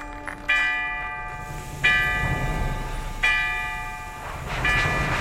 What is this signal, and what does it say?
There is no code.